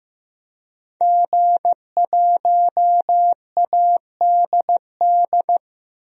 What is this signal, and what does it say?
Morse code, heard as a keyed tone, about 15 wpm, 700 Hz. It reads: G1ADD
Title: DXCC-100 Call Signs - 5 Characters - 15wpm